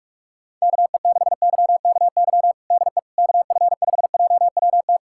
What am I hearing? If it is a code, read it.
Morse code, heard as a keyed tone, about 46 wpm, 700 Hz. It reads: KE6YKY DE KF5JWT
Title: USA Callsign DE Callsign - 6 Characters - 45wpm